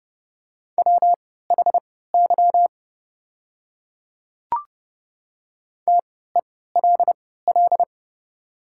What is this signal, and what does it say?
Morse code, heard as a keyed tone, about 31 wpm, spaced as for 17 wpm, 700 Hz. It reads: WHY TELL
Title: 3x Character Spacing - Top 500 Words - 30wpm